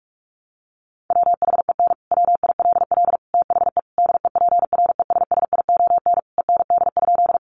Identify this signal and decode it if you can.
Morse code, heard as a keyed tone, about 44 wpm, 700 Hz. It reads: WHEN WILL THE DEPRESSION END?
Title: Sentences from Top 1300 Words 45wpm